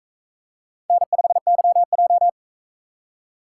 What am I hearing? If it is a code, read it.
Morse code, heard as a keyed tone, about 44 wpm, 700 Hz. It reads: N5YJ